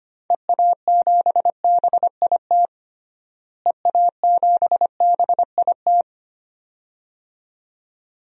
Morse code, heard as a keyed tone, about 25 wpm, 700 Hz. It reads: EA7BIT EA7BIT
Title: DXCC-100 Call Signs - 6 Characters - Repeated 2x - 25wpm